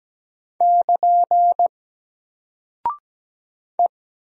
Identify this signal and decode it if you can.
Morse code, heard as a keyed tone, about 17 wpm, 700 Hz. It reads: <KN> E
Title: Single Character (Letter:Number:Punctuation:Prosign) Alt 17wpm